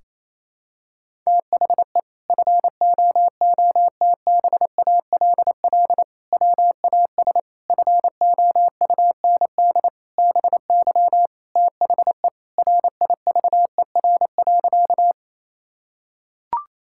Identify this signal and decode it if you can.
Morse code, heard as a keyed tone, about 28 wpm, 700 Hz. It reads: THE FOOTBALL WAS FOUND BY THE RIVER.
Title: Sentences from Top 1600 Words 28wpm